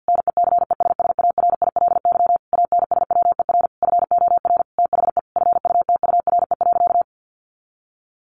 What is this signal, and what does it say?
Morse code, heard as a keyed tone, about 50 wpm, 700 Hz. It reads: NECESSARILY ANSWER FOR THE FUTURE.
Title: Sentences from Top 1400 Words 50wpm